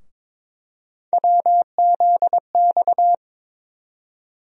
Morse code, heard as a keyed tone, about 22 wpm, 700 Hz. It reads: WZX
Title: USA Call Sign Postfixes - 3 Letters 22wpm